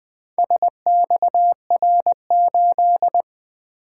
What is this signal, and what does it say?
Morse code, heard as a keyed tone, about 24 wpm, 700 Hz. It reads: SXR8
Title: Speed-Racing - Four Letter-Number Combinations - 20wpm